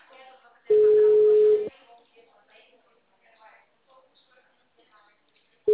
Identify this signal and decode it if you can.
Ringback tone (Europe, 425 Hz), with speech.
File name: out- -2009- - - .183